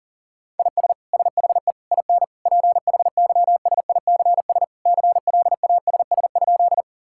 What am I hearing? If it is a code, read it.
Morse code, heard as a keyed tone, about 37 wpm, 700 Hz. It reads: IS SHE IN PHYSICS CLASS?
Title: Sentences from Top 1000 Words - Review Part 5 - 40wpm